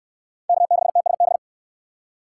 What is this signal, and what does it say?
Morse code, heard as a keyed tone, about 64 wpm, 700 Hz. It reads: K6TSB